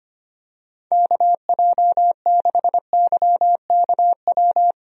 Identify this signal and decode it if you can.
Morse code, heard as a keyed tone, about 25 wpm, 700 Hz. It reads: KJ6YKW
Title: USA Call Signs - 6 Characters - 25wpm